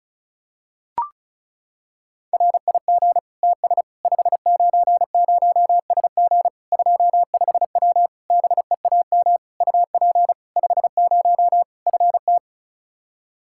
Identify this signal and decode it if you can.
Morse code, heard as a keyed tone, about 34 wpm, 700 Hz. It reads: RIG TS 590SG 25W BEAM UP 50 FT
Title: Practice QSO - Exchange 2 - Info - 35wpm